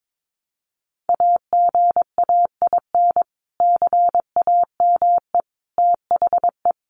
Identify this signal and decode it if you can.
Morse code, heard as a keyed tone, about 22 wpm, 700 Hz. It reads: AGAIN CAME THE
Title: Sentences from Top 500 Words 22wpm